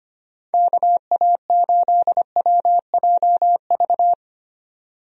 Morse code, heard as a keyed tone, about 25 wpm, 700 Hz. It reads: KA8WJV
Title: USA Call Signs - 6 Characters - Repeated 2x - 25wpm